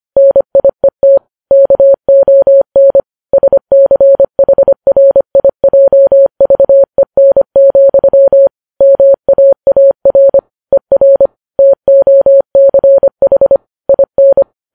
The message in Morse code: NIET KON SCHRIJVEN, MAAR ER TOCH IN